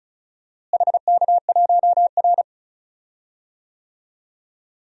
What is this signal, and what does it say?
Morse code, heard as a keyed tone, about 36 wpm, 700 Hz. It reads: HK1R
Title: DXCC-100 Call Signs - 4 Characters - Repeated 3x - 35wpm